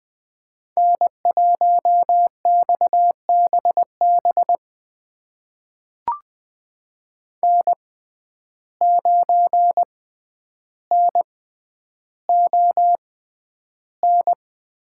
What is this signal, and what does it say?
Morse code, heard as a keyed tone, about 20 wpm, spaced as for 10 wpm, 700 Hz. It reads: N1XBB N9NON
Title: USA Call Signs - 5 Characters - 6x Character Spacing - 20wpm